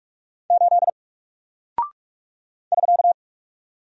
Morse code, heard as a keyed tone, about 46 wpm, 700 Hz. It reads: 8 <SK>
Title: Single Character (Letter:Number:Punctuation:Prosign) Alt 45wpm